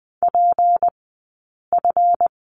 Morse code, heard as a keyed tone, about 20 wpm, spaced as for 8 wpm, 700 Hz. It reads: PF
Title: 151 Lesson 25d - Letter p - Two Characters 20wpm